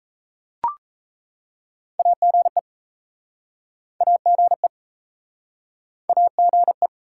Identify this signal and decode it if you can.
Morse code, heard as a keyed tone, about 38 wpm, 700 Hz. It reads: AGE AGE AGE